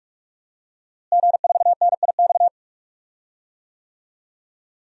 Morse code, heard as a keyed tone, about 44 wpm, 700 Hz. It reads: G4NIX
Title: DXCC-100 Call Signs - 5 Characters - Repeated 3x - 45wpm